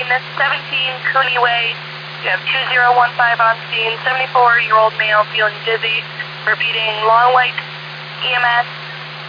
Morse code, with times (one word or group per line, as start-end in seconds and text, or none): none